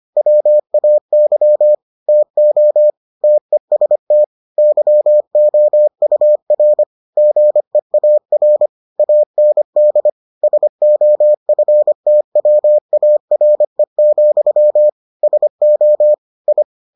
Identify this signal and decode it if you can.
Morse code, heard as a keyed tone, about 25 wpm, 600 Hz. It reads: WAY TO TEST YOUR GEAR AND SOFTWARE, SO I